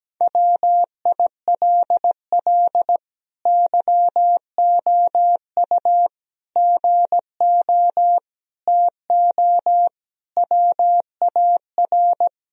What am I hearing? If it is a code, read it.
Morse code, heard as a keyed tone, about 17 wpm, 700 Hz. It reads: WILL YOU GO TO WAR
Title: Sentences from Top 1000 Words - Review Part 2 - 17wpm